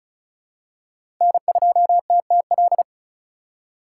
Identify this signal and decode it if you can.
Morse code, heard as a keyed tone, about 34 wpm, 700 Hz. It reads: N2TTL
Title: USA Call Signs - 5 Characters - 2x Character Spacing - 35wpm